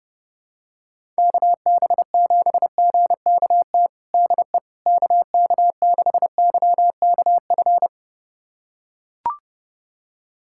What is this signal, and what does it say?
Morse code, heard as a keyed tone, about 30 wpm, 700 Hz. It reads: KB7GKT DE KK6YKF
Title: USA Callsign DE Callsign 30wpm